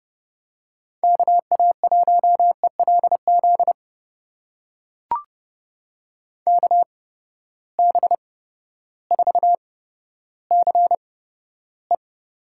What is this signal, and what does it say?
Morse code, heard as a keyed tone, about 30 wpm, spaced as for 14 wpm, 700 Hz. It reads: KA1ELZ KB4CE
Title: USA Call Signs - 8x Character Spacing - 30wpm